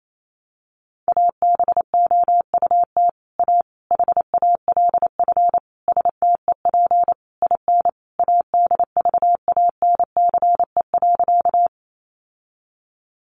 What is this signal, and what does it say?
Morse code, heard as a keyed tone, about 28 wpm, 700 Hz. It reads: ABOUT A HALF STEP IN ADVANCE.
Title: Sentences from Top 1400 Words 28wpm